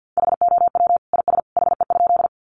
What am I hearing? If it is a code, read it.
Morse code, heard as a keyed tone, about 50 wpm, 700 Hz. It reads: HOW IS HE?
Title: Sentences from Top 100 Words 50wpm